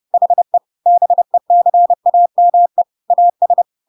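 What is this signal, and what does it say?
Morse code, heard as a keyed tone, about 30 wpm, 700 Hz. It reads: HE BECAME AS
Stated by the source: Chapter 2 of The Adventures of Pinocchio read as a book in Morse code (CW) at 30 wpm, by ik1hge.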